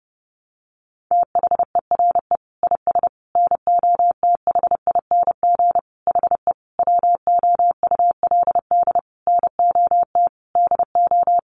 Morse code, heard as a keyed tone, about 30 wpm, 700 Hz. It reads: THERE IS NOTHING HE WOULD NOT DO
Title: Sentences from Top 600 Words 30wpm